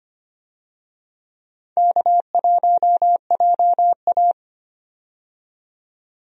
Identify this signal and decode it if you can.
Morse code, heard as a keyed tone, about 25 wpm, 700 Hz. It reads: K1JA
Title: USA Call Signs - 4 Characters - 25wpm